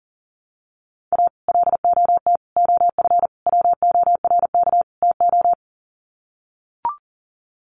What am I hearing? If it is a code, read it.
Morse code, heard as a keyed tone, about 37 wpm, 700 Hz. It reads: A LOT OF WORK TO
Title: Sets of 5 Words Encore2 40wpm